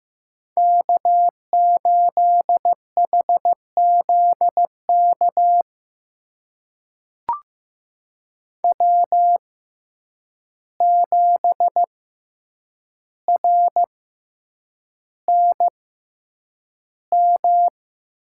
Morse code, heard as a keyed tone, about 15 wpm, spaced as for 8 wpm, 700 Hz. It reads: K8HZK W7RNM
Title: USA Call Signs - 5 Characters - 6x Character Spacing - 15wpm